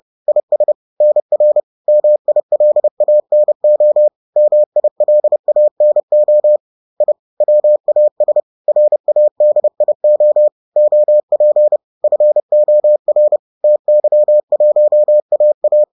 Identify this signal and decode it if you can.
Morse code, heard as a keyed tone, about 30 wpm, 600 Hz. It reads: IS NR MILANO MILANO I WAS RADIO OP FOR TY1AA
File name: 30wpm-7